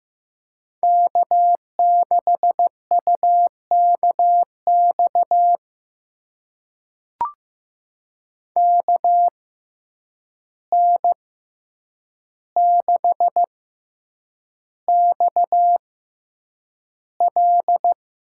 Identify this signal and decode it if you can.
Morse code, heard as a keyed tone, about 15 wpm, spaced as for 8 wpm, 700 Hz. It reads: K6UKX KN6XL
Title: USA Call Signs - 5 Characters - 6x Character Spacing - 15wpm